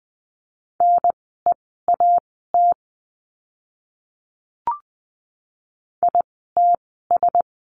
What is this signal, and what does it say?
Morse code, heard as a keyed tone, about 20 wpm, spaced as for 14 wpm, 700 Hz. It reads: NEAT ITS
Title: 054 Lesson 10f - Letter r - Words 20wpm